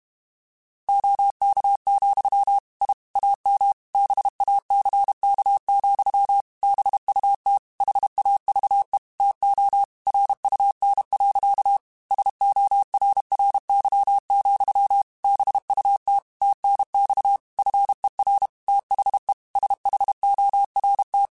Morse code, heard as a keyed tone, about 32 wpm, 800 Hz. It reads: OK, I AM BACK, BUT HAVE TO RUN. SORRY, BUT TNX FER THE SHORT
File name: QSOs / QSOs-mp3s / WA7QOX 2@30